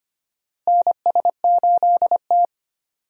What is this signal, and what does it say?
Morse code, heard as a keyed tone, about 25 wpm, 700 Hz. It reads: NS8T
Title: USA Call Signs - 4 Characters - Repeated 2x - 25wpm